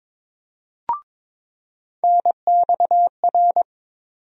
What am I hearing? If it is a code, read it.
Morse code, heard as a keyed tone, about 22 wpm, 700 Hz. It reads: NXR
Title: USA Call Sign Postfixes - 3 Letters 22wpm